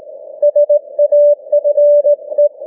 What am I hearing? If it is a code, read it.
Morse code, heard as a keyed tone, about 18 wpm, 590 Hz. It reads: SAFE